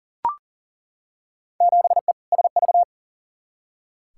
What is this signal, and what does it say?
Morse code, heard as a keyed tone, about 37 wpm, 700 Hz. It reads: 7E SV
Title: Field Day Exchanges Encore - 40wpm